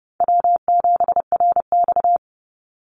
Morse code, heard as a keyed tone, about 30 wpm, 700 Hz. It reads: W7RX
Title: Call Signs Repeated 2x - 30wpm